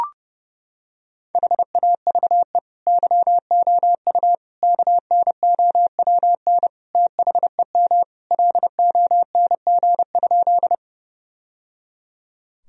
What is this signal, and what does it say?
Morse code, heard as a keyed tone, about 32 wpm, 700 Hz. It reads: HAVE YOU KNOWN THEM LONG?
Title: Sentences from Top 1000 Words - Review Part 3 - 30wpm